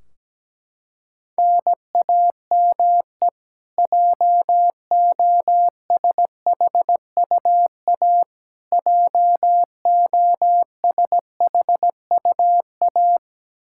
Morse code, heard as a keyed tone, about 17 wpm, 700 Hz. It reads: NAME JOSHUA JOSHUA
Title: Practice QSO - Exchange 1 - Beginning - 17wpm